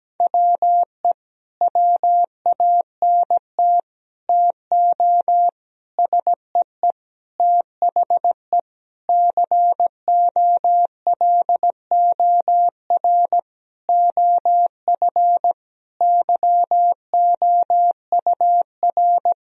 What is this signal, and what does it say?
Morse code, heard as a keyed tone, about 17 wpm, 700 Hz. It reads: WE WANT TO SEE THE COLOR OF YOUR
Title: Sentences from Top 1000 Words - Review Part 2 - 17wpm